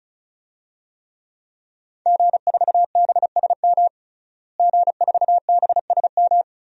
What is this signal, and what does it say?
Morse code, heard as a keyed tone, about 36 wpm, 700 Hz. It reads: G4BSM G4BSM
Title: DXCC-100 Call Signs - 5 Characters - Repeated 3x - 35wpm